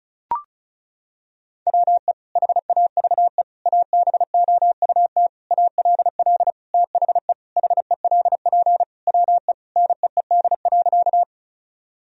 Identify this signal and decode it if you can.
Morse code, heard as a keyed tone, about 36 wpm, 700 Hz. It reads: WE HAVE ABOUT ALL THE HELP WE NEED.
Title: Sentences from Top 1000 Words - Review Part 1 - 35wpm